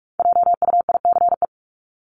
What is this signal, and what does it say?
Morse code, heard as a keyed tone, about 44 wpm, 700 Hz. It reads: JUICE
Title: Top 1251-1500 Words 45wpm